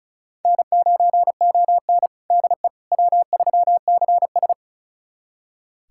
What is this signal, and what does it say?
Morse code, heard as a keyed tone, about 36 wpm, 700 Hz. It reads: N9ON DE W3CS
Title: USA Callsign DE Callsign - 4 Characters - 35wpm